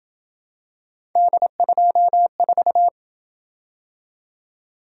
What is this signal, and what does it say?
Morse code, heard as a keyed tone, about 26 wpm, 700 Hz. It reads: D24